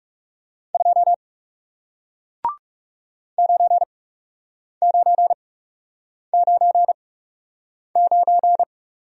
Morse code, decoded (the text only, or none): 2 9999